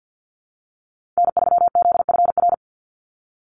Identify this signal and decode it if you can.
Morse code, heard as a keyed tone, about 50 wpm, 700 Hz. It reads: N3ZFR
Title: USA Call Signs 50wpm